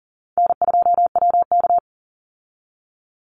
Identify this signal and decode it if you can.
Morse code, heard as a keyed tone, about 40 wpm, 700 Hz. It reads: N2WK
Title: Easy Call Signs Repeated 2x - 40wpm